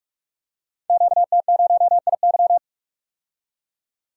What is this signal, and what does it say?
Morse code, heard as a keyed tone, about 44 wpm, 700 Hz. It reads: QT0IY
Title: Code Groups - 5 Letter:Numbers - 45wpm